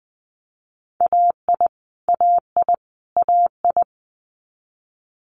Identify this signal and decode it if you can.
Morse code, heard as a keyed tone, about 20 wpm, 700 Hz. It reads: AI AI AI
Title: Multi-letter Phonemes Repeated 3x 20wpm